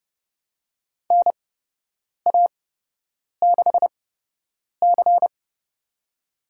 Morse code, heard as a keyed tone, about 30 wpm, spaced as for 8 wpm, 700 Hz. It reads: NA6C
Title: USA Call Signs - 4 Characters - 8x Character Spacing - 30wpm